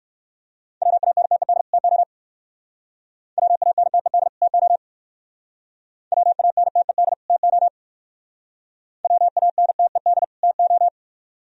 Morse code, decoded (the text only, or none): WANTED TO WANTED TO WANTED TO WANTED TO